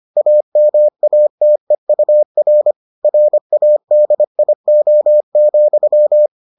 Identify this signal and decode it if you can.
Morse code, heard as a keyed tone, about 25 wpm, 600 Hz. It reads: AMATEUR RADIO,